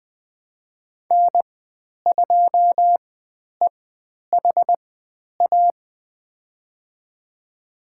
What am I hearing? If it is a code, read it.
Morse code, heard as a keyed tone, about 20 wpm, spaced as for 10 wpm, 700 Hz. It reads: N2EHA
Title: 109 Lesson 18g - Letter m - Call Signs 20wpm